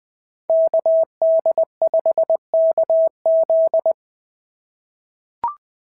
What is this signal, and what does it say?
Morse code, heard as a keyed tone, about 20 wpm, 690 Hz. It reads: KD5KZ
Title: SOTA QSO Practice 20wpm